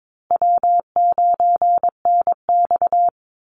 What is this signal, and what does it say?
Morse code, heard as a keyed tone, about 22 wpm, 700 Hz. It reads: W9NX
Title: Easy Call Signs Repeated 2x - 22wpm